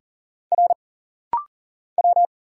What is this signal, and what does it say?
Morse code, heard as a keyed tone, about 40 wpm, 700 Hz. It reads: R W
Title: Single Letters - Rapid-Fire 40wpm